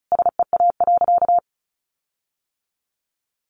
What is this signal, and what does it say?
Morse code, heard as a keyed tone, about 34 wpm, 700 Hz. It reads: SEA.
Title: Sentences from Top 1300 Words 35wpm